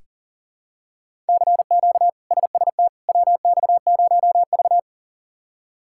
Morse code, heard as a keyed tone, about 40 wpm, 700 Hz. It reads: CQ SST WX0V
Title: K1USN SST QSO Practice - 40wpm